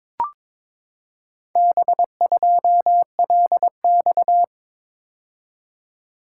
Morse code, heard as a keyed tone, about 22 wpm, 700 Hz. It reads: B2LX
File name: practice / 22wpm / Four Letter-Number Combinations - 22wpm